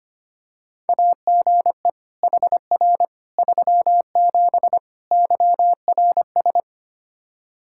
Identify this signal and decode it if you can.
Morse code, heard as a keyed tone, about 25 wpm, 700 Hz. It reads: AGE HR 37 YRS
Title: Practice QSO - Exchange 3 - More Info - 25wpm